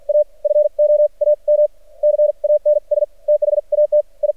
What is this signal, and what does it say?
Morse code, heard as a keyed tone, about 50 wpm, spaced as for 41 wpm, 600 Hz. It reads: A FOAM CANS THAT I
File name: archived / CRSnet